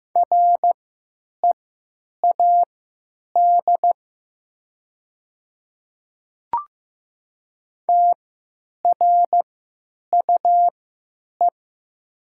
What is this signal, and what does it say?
Morse code, heard as a keyed tone, about 15 wpm, spaced as for 9 wpm, 700 Hz. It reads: READ TRUE
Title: 3x Character Spacing - Top 500 Words - 15wpm